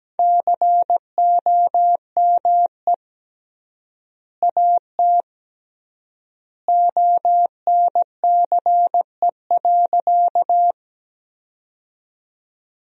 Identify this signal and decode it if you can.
Morse code, heard as a keyed tone, about 17 wpm, 700 Hz. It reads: COME AT ONCE.
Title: 3x Word-Spacing - Sentences from Top 300 Words - 17wpm